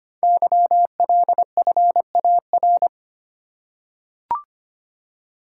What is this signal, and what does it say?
Morse code, heard as a keyed tone, about 25 wpm, 700 Hz. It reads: YLFAR
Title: Code Groups - 5 Letters - 25wpm